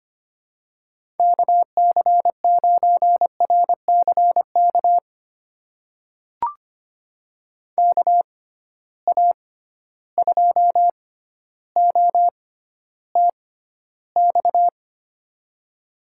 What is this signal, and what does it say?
Morse code, heard as a keyed tone, about 25 wpm, spaced as for 13 wpm, 700 Hz. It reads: KC9RCK KA2OTX
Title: USA Call Signs - 6 Characters - 6x Character Spacing - 25wpm